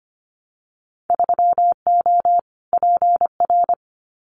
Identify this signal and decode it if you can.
Morse code, heard as a keyed tone, about 25 wpm, 700 Hz. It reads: 3O PR
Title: Winter Field Day Exchanges Encore 25wpm